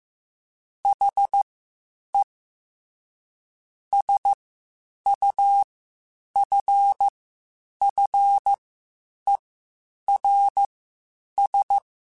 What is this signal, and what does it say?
Morse code, heard as a keyed tone, about 15 wpm, spaced as for 8 wpm, 800 Hz. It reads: HE SUFFERS
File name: Quotes Spaced / Quotes Spaced-mp3s / HoofInMouth sp @13